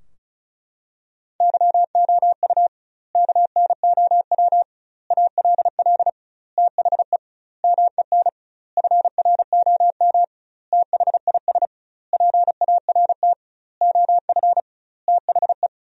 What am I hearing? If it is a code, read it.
Morse code, heard as a keyed tone, about 36 wpm, 700 Hz. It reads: YOU KNOW ALL THE MEN FROM THIS PART OF THE
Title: 2x Word-Spacing - Sentences from Top 300 Words - 35wpm